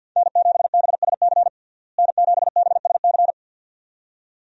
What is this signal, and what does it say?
Morse code, heard as a keyed tone, about 53 wpm, 700 Hz. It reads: N7BSC N7BSC
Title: USA Call Signs - 5 Characters - Repeated 2x - 50wpm